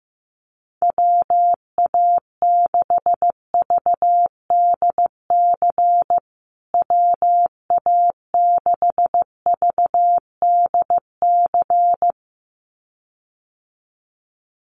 Morse code, heard as a keyed tone, about 15 wpm, 700 Hz. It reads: WA6VDC WA6VDC
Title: USA Call Signs Repeated 2x 15wpm